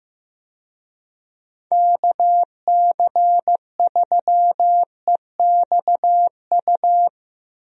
Morse code, heard as a keyed tone, about 15 wpm, 700 Hz. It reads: KC3EXU
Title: USA Call Signs - 6 Characters - Repeated 2x - 15wpm